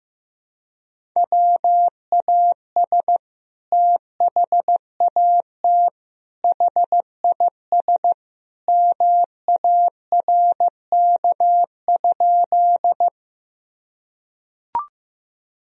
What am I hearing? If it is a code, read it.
Morse code, heard as a keyed tone, about 16 wpm, 700 Hz. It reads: WAS THAT HIS MARK?
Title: Sentences from Top 1000 Words - Review Part 1 - 15wpm